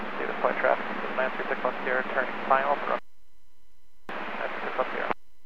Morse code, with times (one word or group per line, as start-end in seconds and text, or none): none